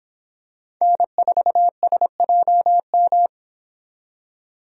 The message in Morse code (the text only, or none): N4SJM